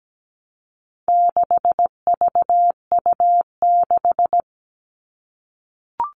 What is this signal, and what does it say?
Morse code, heard as a keyed tone, about 17 wpm, 700 Hz. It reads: 6VU6
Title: All Cut Numbers - 4-Digit 17wpm